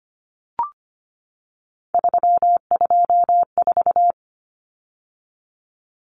Morse code, heard as a keyed tone, about 25 wpm, 700 Hz. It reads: 324
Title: Numbers - 3-Digit 25wpm